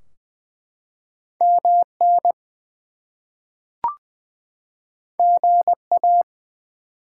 Morse code, heard as a keyed tone, about 20 wpm, 700 Hz. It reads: MN GA
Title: US States and Canadian Provinces 20wpm